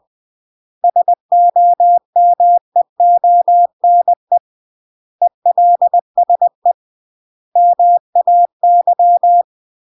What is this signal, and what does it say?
Morse code, heard as a keyed tone, about 20 wpm, 700 Hz. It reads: SOMEONE ELSE MAY